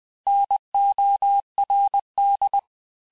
Morse code, heard as a keyed tone, about 20 wpm, 800 Hz. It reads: NORD
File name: cours-du-mercredi / audios / cours-870-FIN-de-QSO-vitesse-19-MPM